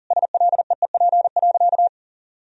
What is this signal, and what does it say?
Morse code, heard as a keyed tone, about 40 wpm, 700 Hz. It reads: SLEEP.
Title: Sentences from Top 500 Words and Abbreviations 40wpm